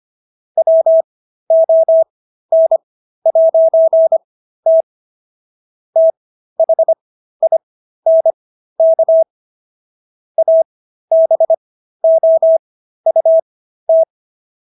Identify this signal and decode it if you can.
Morse code, heard as a keyed tone, about 25 wpm, spaced as for 13 wpm, 650 Hz. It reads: WON'T THINK ABOUT